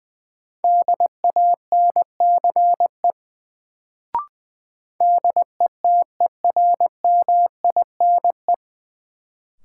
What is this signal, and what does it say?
Morse code, heard as a keyed tone, about 20 wpm, 700 Hz. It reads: DANCE DETERMINE
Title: Sentences from Top 700 Words - New Word Drill - 20wpm